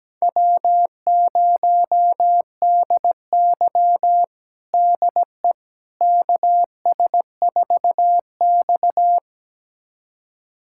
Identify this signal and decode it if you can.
Morse code, heard as a keyed tone, about 17 wpm, 700 Hz. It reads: W0DY DE KS4X
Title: USA Callsign DE Callsign - 4 Characters - 17wpm